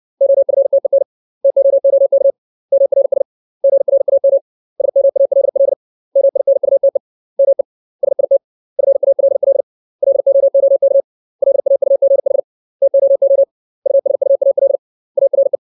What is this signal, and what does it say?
Morse code, heard as a keyed tone, about 60 wpm, 550 Hz. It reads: OFTEN TOOK MRS GRAM SMALL MINUTE ME SIT FALL LOOK LAUGH TOO USUAL ARE
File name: 632 words / 632 Words 60WPM